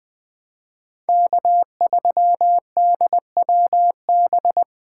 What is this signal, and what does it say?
Morse code, heard as a keyed tone, about 21 wpm, 700 Hz. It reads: K3DWB
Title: Speed Racing - USA 1x3 Call Signs 17wpm